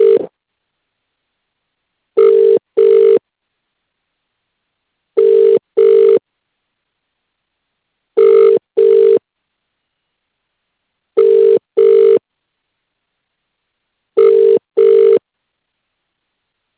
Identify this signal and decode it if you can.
Ringback tone (UK), 400 + 450 Hz.